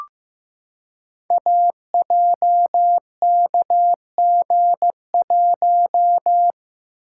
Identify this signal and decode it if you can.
Morse code, heard as a keyed tone, about 15 wpm, 700 Hz. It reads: AJKG1
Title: Code Groups - 5 Letter:Numbers - 15wpm